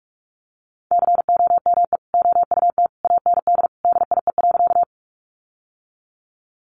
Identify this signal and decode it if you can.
Morse code, heard as a keyed tone, about 46 wpm, 700 Hz. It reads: COME OUT AND DIE.
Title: Sentences from Top 1300 Words 45wpm